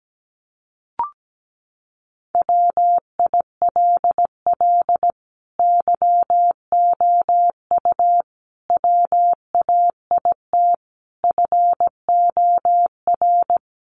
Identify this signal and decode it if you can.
Morse code, heard as a keyed tone, about 17 wpm, 700 Hz. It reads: WILL YOU WAIT FOR
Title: Sentences from Top 900 Words 17wpm